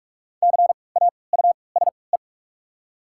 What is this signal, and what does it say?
Morse code, heard as a keyed tone, about 46 wpm, spaced as for 26 wpm, 700 Hz. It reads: CAUSE
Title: 3x Character Spacing - Top 500 Words - 45wpm